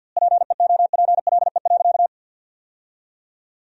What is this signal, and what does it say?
Morse code, heard as a keyed tone, about 52 wpm, 700 Hz. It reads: PEOPLE.
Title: 2x Word-Spacing - Sentences from Top 400 Words - 50wpm